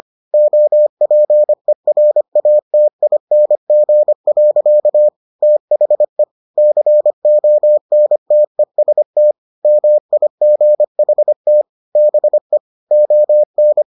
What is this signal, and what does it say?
Morse code, heard as a keyed tone, about 25 wpm, 600 Hz. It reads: OPERATING. THE CONTEST MIGHT BE ON